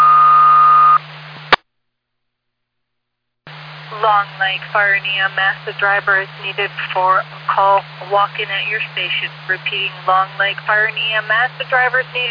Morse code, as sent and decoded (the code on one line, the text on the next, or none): none